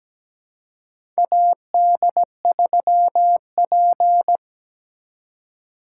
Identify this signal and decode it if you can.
Morse code, heard as a keyed tone, about 17 wpm, 700 Hz. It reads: AD3P
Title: USA Call Signs - 4 Characters - 4x Character Spacing - 17wpm